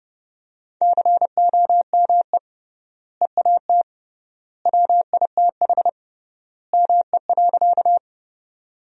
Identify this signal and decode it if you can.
Morse code, heard as a keyed tone, about 30 wpm, 700 Hz. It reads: COME EAT WITH ME.
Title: 3x Word-Spacing - Sentences from Top 300 Words - 30wpm